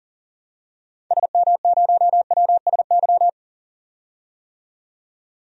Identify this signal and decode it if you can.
Morse code, heard as a keyed tone, about 40 wpm, 700 Hz. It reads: SM0WSY
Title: DXCC-100 Call Signs - 6 Characters - 40wpm